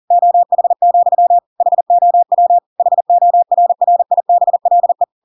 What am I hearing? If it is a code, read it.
Morse code, heard as a keyed tone, about 40 wpm, 700 Hz. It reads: OH, HOW HORRIBLE
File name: Pinocchio / Pinocchio ch5 40 40 ik1hge